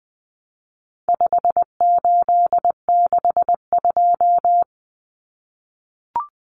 Morse code, heard as a keyed tone, about 20 wpm, 700 Hz. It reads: 5862
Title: Common Cut Numbers - 4-Digit 20wpm